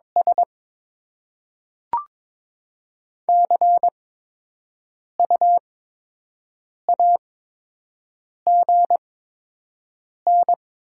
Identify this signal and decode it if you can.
Morse code, heard as a keyed tone, about 22 wpm, spaced as for 6 wpm, 700 Hz. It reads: S CU AGN
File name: Top 20 QSO Elements - 8x Character Spacing - 22wpm